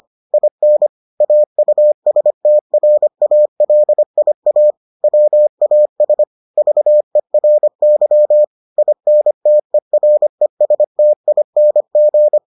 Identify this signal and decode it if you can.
Morse code, heard as a keyed tone, about 25 wpm, 600 Hz. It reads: IN AUSTRALIA WAS VERY INTERESTING